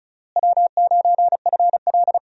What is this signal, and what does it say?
Morse code, heard as a keyed tone, about 36 wpm, 700 Hz. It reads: W9FL
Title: USA Call Signs - 4 Characters - Repeated 2x - 35wpm